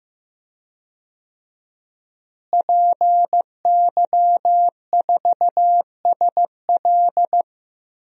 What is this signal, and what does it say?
Morse code, heard as a keyed tone, about 16 wpm, 700 Hz. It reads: PY4SL
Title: DXCC-100 Call Signs - 5 Characters - Repeated 3x - 15wpm